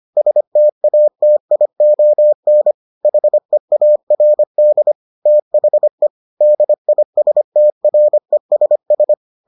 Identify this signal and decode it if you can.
Morse code, heard as a keyed tone, about 25 wpm, 600 Hz. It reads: STATION HEARD THE DISTRESS